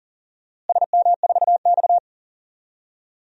Morse code, heard as a keyed tone, about 37 wpm, 700 Hz. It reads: SM4X
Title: DXCC-100 Call Signs - 4 Characters - Repeated 3x - 40wpm